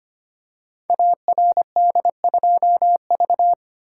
Morse code, heard as a keyed tone, about 25 wpm, 700 Hz. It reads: ARD2V
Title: Code Groups - 5 Letter:Numbers - 25wpm